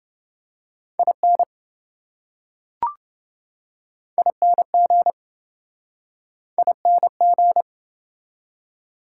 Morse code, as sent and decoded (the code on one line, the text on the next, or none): .. -. / .. -. --. / .. -. --.
IN ING ING